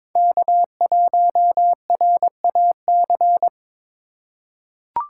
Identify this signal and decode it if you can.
Morse code, heard as a keyed tone, about 21 wpm, 700 Hz. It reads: K1RAC
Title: USA Call Signs - 5 Characters - 4x Character Spacing - 22wpm